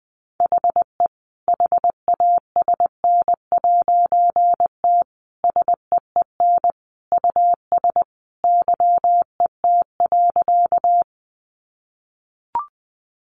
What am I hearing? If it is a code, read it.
Morse code, heard as a keyed tone, about 19 wpm, 700 Hz. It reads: HE HASN'T SEEN US YET.
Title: A Sound Of Thunder 20wpm